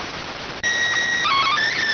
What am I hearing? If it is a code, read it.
SSTV (slow-scan television), PD 290.